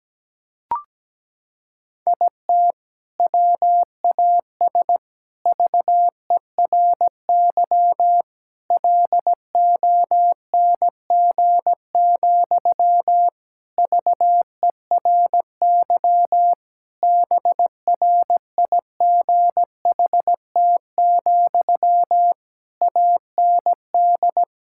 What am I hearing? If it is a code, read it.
Morse code, heard as a keyed tone, about 17 wpm, 700 Hz. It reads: IT WAS VERY LONG, VERY BRIGHT, AND
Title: Sentences from Top 1600 Words 17wpm